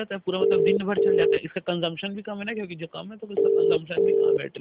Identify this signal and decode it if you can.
Ringback tone (UK), 400 + 450 Hz, with speech.